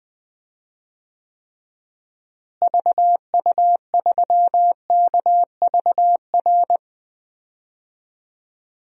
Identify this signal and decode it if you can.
Morse code, heard as a keyed tone, about 20 wpm, 700 Hz. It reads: VU3KVR
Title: DXCC-100 Call Signs - 6 Characters - Repeated 3x - 20wpm